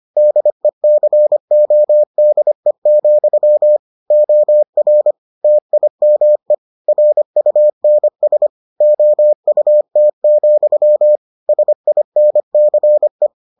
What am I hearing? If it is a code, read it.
Morse code, heard as a keyed tone, about 25 wpm, 600 Hz. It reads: DECODE, OR TIME RUNS OUT, SINCE